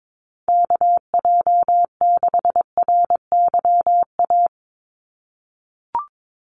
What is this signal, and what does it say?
Morse code, heard as a keyed tone, about 22 wpm, 700 Hz. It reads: KJ6RYA
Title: USA Call Signs 22wpm